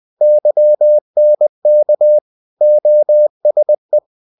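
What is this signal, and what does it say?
Morse code, heard as a keyed tone, about 20 wpm, 600 Hz. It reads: YNK OSE
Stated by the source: Morse code practice of short Danish words at 20 wpm, from CW med Gnister.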